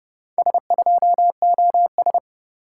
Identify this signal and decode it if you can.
Morse code, heard as a keyed tone, about 29 wpm, 700 Hz. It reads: S2OS